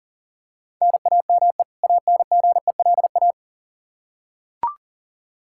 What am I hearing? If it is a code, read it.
Morse code, heard as a keyed tone, about 37 wpm, 700 Hz. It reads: NAME ANGELA
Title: K1USN SST QSO Practice 40wpm